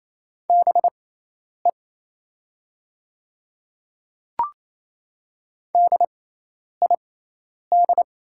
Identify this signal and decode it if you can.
Morse code, heard as a keyed tone, about 28 wpm, spaced as for 10 wpm, 700 Hz. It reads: BE DID
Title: 6x Character Spacing - Top 100 Words - 28wpm